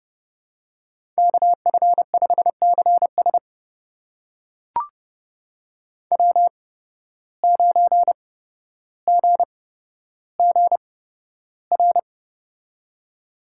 Morse code, heard as a keyed tone, about 31 wpm, spaced as for 13 wpm, 700 Hz. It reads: KF5CS W9GGR
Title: USA Call Signs - 5 Characters - 8x Character Spacing - 30wpm